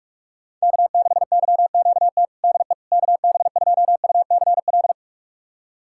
Morse code, heard as a keyed tone, about 46 wpm, 700 Hz. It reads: K6YQT DE KB2UCL
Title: USA Callsign DE Callsign 45wpm